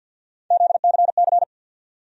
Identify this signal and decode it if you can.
Morse code, heard as a keyed tone, about 53 wpm, 700 Hz. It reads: ZCC
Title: USA Call Sign Postfixes - 3 Letters 50wpm